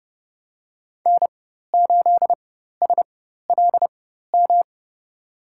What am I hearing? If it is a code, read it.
Morse code, heard as a keyed tone, about 30 wpm, spaced as for 14 wpm, 700 Hz. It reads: N8SLM
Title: USA Call Signs - 5 Characters - 4x Character Spacing - 30wpm